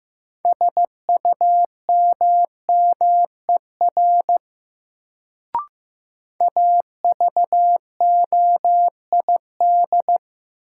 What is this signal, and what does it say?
Morse code, heard as a keyed tone, about 15 wpm, 700 Hz. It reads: SUMMER AVOID
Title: Sentences from Top 1000 Words - Review Part 6 - New Word Drill - 15wpm